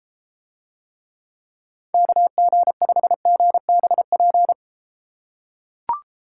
Morse code, heard as a keyed tone, about 33 wpm, 700 Hz. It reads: KG5GBP